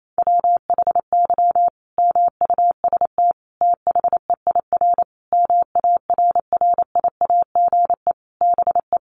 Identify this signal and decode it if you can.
Morse code, heard as a keyed tone, about 28 wpm, 700 Hz. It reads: WHY MUST THEIR MARRIAGE BE
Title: Sentences from Top 1000 Words 28wpm